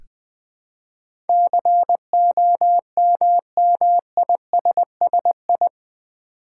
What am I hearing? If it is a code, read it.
Morse code, heard as a keyed tone, about 20 wpm, 700 Hz. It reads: COMMISSI
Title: Build A Word - 4 Letters At A Time - 10-Letter Words - 20wpm